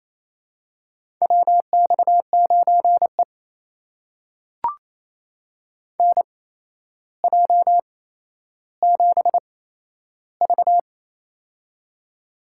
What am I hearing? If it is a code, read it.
Morse code, heard as a keyed tone, about 28 wpm, spaced as for 12 wpm, 700 Hz. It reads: WX9E NJ7V
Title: USA Call Signs - 4 Characters - 8x Character Spacing - 28wpm